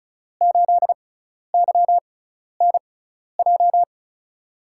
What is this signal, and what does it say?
Morse code, heard as a keyed tone, about 36 wpm, spaced as for 12 wpm, 700 Hz. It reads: 8YNJ
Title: USA Call Signs - 6 Characters - 6x Character Spacing - 35wpm